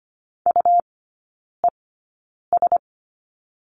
Morse code, heard as a keyed tone, about 25 wpm, spaced as for 9 wpm, 700 Hz. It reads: UES
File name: Lesson 17e - Letter u - Three Characters 25wpm